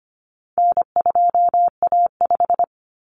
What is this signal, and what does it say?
Morse code, heard as a keyed tone, about 25 wpm, 700 Hz. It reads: N2A5
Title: Common Cut Numbers - 4-Digit 25wpm